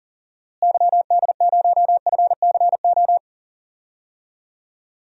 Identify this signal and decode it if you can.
Morse code, heard as a keyed tone, about 40 wpm, 700 Hz. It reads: YD0FCO
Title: DXCC-100 Call Signs - 6 Characters - Repeated 2x - 40wpm